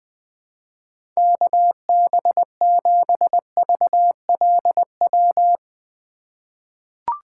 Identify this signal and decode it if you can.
Morse code, heard as a keyed tone, about 20 wpm, 700 Hz. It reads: KB7VLW
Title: USA Call Signs - 6 Characters - 4x Character Spacing - 20wpm